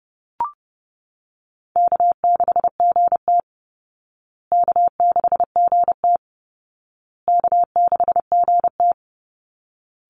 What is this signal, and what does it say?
Morse code, heard as a keyed tone, about 30 wpm, 700 Hz. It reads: K6GT K6GT K6GT
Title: Call Signs Repeated 2x - 30wpm